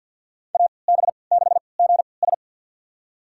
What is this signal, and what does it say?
Morse code, heard as a keyed tone, about 50 wpm, spaced as for 28 wpm, 700 Hz. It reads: AB6GS
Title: USA Call Signs - 5 Characters - 3x Character Spacing - 50wpm